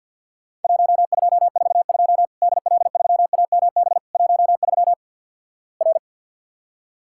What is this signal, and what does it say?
Morse code, heard as a keyed tone, about 52 wpm, 700 Hz. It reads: 1242 DL3AMB 13 R
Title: Worked All Europe DX Contest - QTC Exchange with Repeats - 50wpm